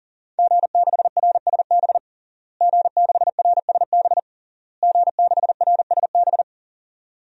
Morse code, heard as a keyed tone, about 37 wpm, 700 Hz. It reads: G6RSB G6RSB G6RSB
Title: DXCC-100 Call Signs - 5 Characters - Repeated 3x - 40wpm